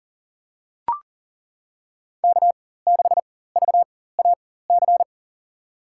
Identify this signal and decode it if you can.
Morse code, heard as a keyed tone, about 40 wpm, spaced as for 19 wpm, 700 Hz. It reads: K6VAC
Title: USA Call Signs - 4x Character Spacing - 40wpm